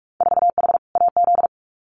Morse code, heard as a keyed tone, about 46 wpm, 700 Hz. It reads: 4H AZ
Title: Winter Field Day Exchanges Encore 45wpm